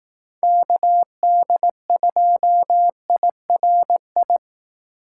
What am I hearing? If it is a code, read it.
Morse code, heard as a keyed tone, about 18 wpm, 700 Hz. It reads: KD2IRI
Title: Speed Racing - USA 2x3 Call Signs 15wpm